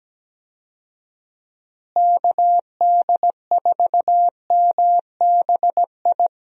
Morse code, heard as a keyed tone, about 17 wpm, 700 Hz. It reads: KD4MBI
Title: USA Call Signs - 6 Characters - Repeated 2x - 17wpm